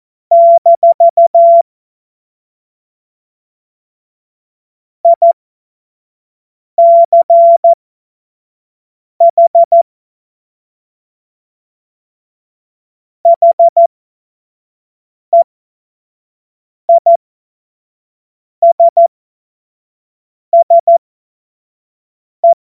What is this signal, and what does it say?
Morse code, heard as a keyed tone, about 14 wpm, spaced as for 5 wpm, 680 Hz. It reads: - ICH HEISSE
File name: Morsekurs / Witze / Witz 30. Dezember 2025 25 F70